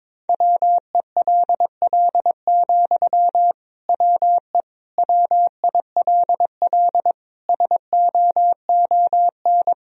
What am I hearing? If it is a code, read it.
Morse code, heard as a keyed tone, about 22 wpm, 700 Hz. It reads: WELL, WE WILL SOON
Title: Sentences from Top 1000 Words - Review Part 4 - 22wpm